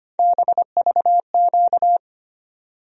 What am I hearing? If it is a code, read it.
Morse code, heard as a keyed tone, about 25 wpm, 700 Hz. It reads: BVQ
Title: USA Call Sign Postfixes - 3 Letters 25wpm